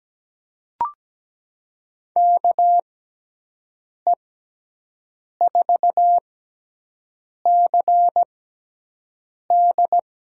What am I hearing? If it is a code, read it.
Morse code, heard as a keyed tone, about 17 wpm, spaced as for 6 wpm, 700 Hz. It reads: KE4CD
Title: USA Call Signs - 5 Characters - 6x Character Spacing - 17wpm